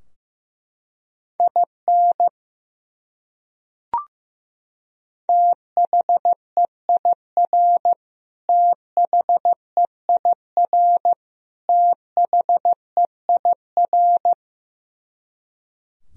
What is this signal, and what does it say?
Morse code, heard as a keyed tone, about 15 wpm, 700 Hz. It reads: IN THEIR THEIR THEIR
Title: Top 100 Words Repeated 15wpm